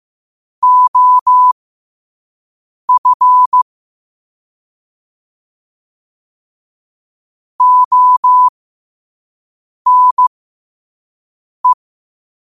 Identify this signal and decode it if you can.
Morse code, heard as a keyed tone, about 15 wpm, spaced as for 5 wpm, 1000 Hz. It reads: OF ONE
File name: qotdfiles / 5wpm